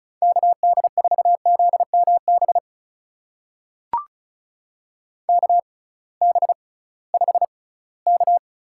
Morse code, heard as a keyed tone, about 34 wpm, spaced as for 20 wpm, 700 Hz. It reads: KD4ZMB KB5K
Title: USA Call Signs - 6 Characters - 6x Character Spacing - 35wpm